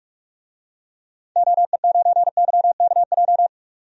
Morse code, heard as a keyed tone, about 46 wpm, 700 Hz. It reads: OE9YKJ
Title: DXCC-100 Call Signs - 6 Characters - 45wpm